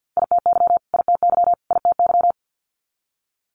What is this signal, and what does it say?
Morse code, heard as a keyed tone, about 50 wpm, 700 Hz. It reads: ITY ITY ITY
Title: Multi-letter Phonemes Repeated 3x 50wpm